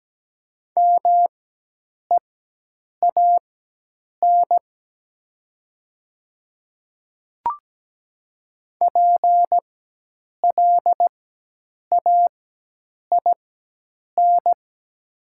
Morse code, heard as a keyed tone, about 17 wpm, spaced as for 8 wpm, 700 Hz. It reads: MEAN PLAIN
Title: 4x Character Spacing - Top 500 Words - 17wpm